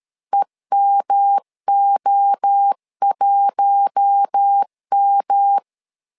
Morse code, heard as a keyed tone, about 13 wpm, 800 Hz. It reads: EMO1M
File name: LEZIONI / 4 Q1 12wpm 2028